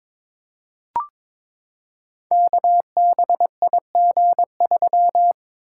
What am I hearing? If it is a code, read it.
Morse code, heard as a keyed tone, about 21 wpm, 700 Hz. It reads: KBIG3
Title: Code Groups - 5 Letter:Numbers - 22wpm